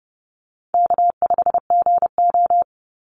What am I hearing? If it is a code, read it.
Morse code, heard as a keyed tone, about 30 wpm, 700 Hz. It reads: K5GO
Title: Call Signs Repeated 2x - 30wpm